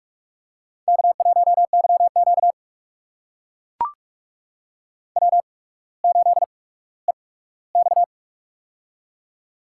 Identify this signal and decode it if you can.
Morse code, heard as a keyed tone, about 44 wpm, spaced as for 19 wpm, 700 Hz. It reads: K1YQ W8EX
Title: USA Call Signs - 4 Characters - 8x Character Spacing - 45wpm